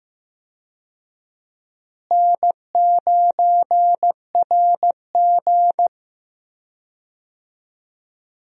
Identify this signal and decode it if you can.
Morse code, heard as a keyed tone, about 15 wpm, 700 Hz. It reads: N9RG
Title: USA Call Signs - 4 Characters - Repeated 3x - 15wpm